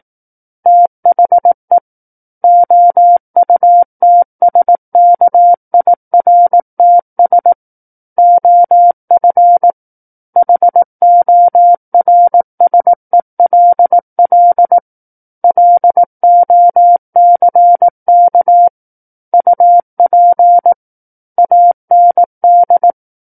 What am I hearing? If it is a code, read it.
Morse code, heard as a keyed tone, about 18 wpm, 700 Hz. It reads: THE OUTSKIRTS OF HORSELL LOCK UP AND
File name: Book One / 09-Chapter 9 - 18 WPM